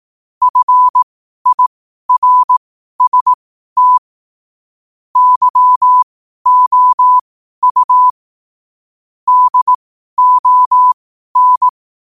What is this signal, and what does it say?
Morse code, heard as a keyed tone, about 18 wpm, spaced as for 12 wpm, 1000 Hz. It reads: FIRST YOU DON
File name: qotdfiles / 12wpm